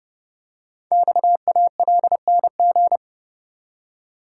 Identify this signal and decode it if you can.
Morse code, heard as a keyed tone, about 30 wpm, 700 Hz. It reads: XALNG
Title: Code Groups - 5 Letters - 30wpm